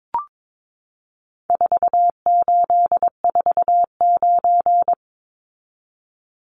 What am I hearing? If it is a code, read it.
Morse code, heard as a keyed tone, about 22 wpm, 700 Hz. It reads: 4849
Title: Numbers - 4-Digit 22wpm